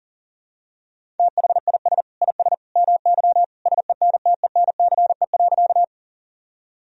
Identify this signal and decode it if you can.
Morse code, heard as a keyed tone, about 37 wpm, 700 Hz. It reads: THIS IS MY SENTENCE.
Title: Sentences from Top 1000 Words - Review Part 1 - 40wpm